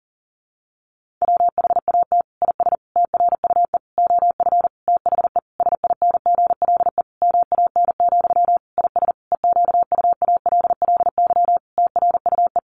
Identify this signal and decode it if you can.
Morse code, heard as a keyed tone, about 40 wpm, 700 Hz. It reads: WHAT IS TRUE OF THE SINGLE MAN, IS EQUALLY TRUE
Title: Sentences from Top 1400 Words 40wpm